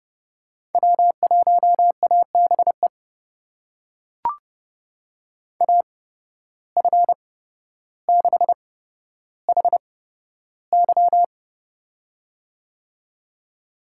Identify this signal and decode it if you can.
Morse code, heard as a keyed tone, about 30 wpm, spaced as for 13 wpm, 700 Hz. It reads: W1ABE AF6HY
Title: USA Call Signs - 5 Characters - 8x Character Spacing - 30wpm